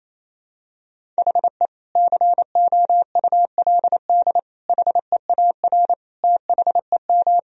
Morse code, heard as a keyed tone, about 28 wpm, 700 Hz. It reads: HE COULD HEAR THEM
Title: Sentences from Top 1000 Words - Review Part 1 - 28wpm